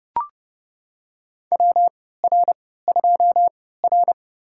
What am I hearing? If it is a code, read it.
Morse code, heard as a keyed tone, about 30 wpm, spaced as for 17 wpm, 700 Hz. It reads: WR2R
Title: USA Call Signs - 4 Characters - 3x Character Spacing - 30wpm